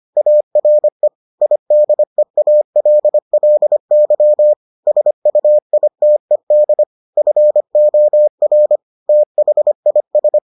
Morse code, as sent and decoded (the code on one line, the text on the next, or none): .- .-. / .. -.. .- .-.. .-.. -.-- / ... ..- .. - . -.. / ..-. --- .-. / - .... .. ...
ARE IDEALLY SUITED FOR THIS